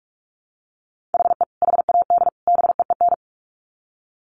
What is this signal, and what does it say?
Morse code, heard as a keyed tone, about 44 wpm, 700 Hz. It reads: HE HAD BEEN
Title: Sets of 3 Words 45wpm